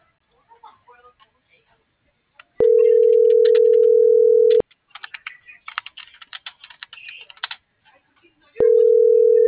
Ringback tone (North America), 440 + 480 Hz.